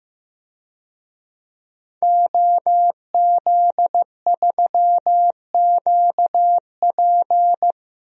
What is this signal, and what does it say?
Morse code, heard as a keyed tone, about 15 wpm, 700 Hz. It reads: OZ3QP
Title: DXCC-100 Call Signs - 5 Characters - Repeated 3x - 15wpm